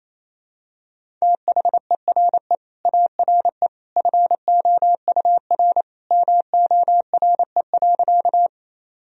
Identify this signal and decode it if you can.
Morse code, heard as a keyed tone, about 27 wpm, 700 Hz. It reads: THERE ARE FOUR MORE.
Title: Sentences from Top 1000 Words - Review Part 1 - 28wpm